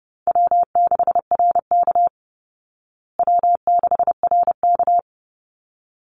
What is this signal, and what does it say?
Morse code, heard as a keyed tone, about 30 wpm, 700 Hz. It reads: W6RK W6RK
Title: Call Signs Repeated 3x - 30wpm